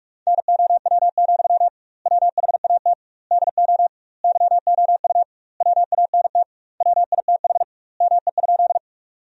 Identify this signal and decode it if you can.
Morse code, heard as a keyed tone, about 46 wpm, 700 Hz. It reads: NOW, WHAT DO YOU WANT WITH ME?
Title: 2x Word-Spacing - Sentences from Top 200 Words - 45wpm